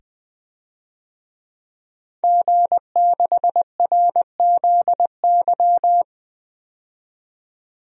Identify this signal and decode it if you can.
Morse code, heard as a keyed tone, about 20 wpm, 700 Hz. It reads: G6RZY
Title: DXCC-100 Call Signs - 5 Characters - 20wpm